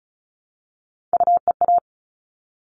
Morse code, heard as a keyed tone, about 34 wpm, 700 Hz. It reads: UEA
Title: All Cut Numbers - 3-Digit 35wpm